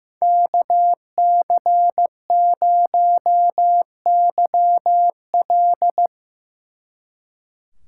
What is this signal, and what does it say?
Morse code, heard as a keyed tone, about 17 wpm, 700 Hz. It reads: KC0YL
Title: Speed Racing - USA 2x2 Call Signs 15wpm